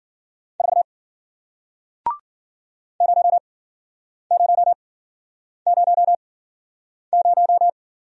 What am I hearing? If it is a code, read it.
Morse code, heard as a keyed tone, about 53 wpm, spaced as for 9 wpm, 700 Hz. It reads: 4 0000